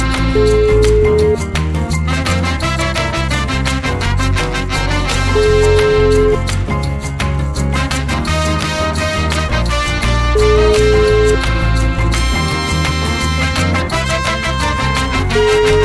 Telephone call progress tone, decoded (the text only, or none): ringback tone